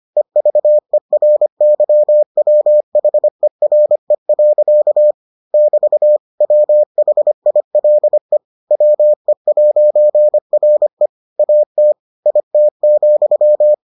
Morse code, heard as a keyed tone, about 25 wpm, 600 Hz. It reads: EVERYWHERE. <BT> WHILE WE'RE AT IT,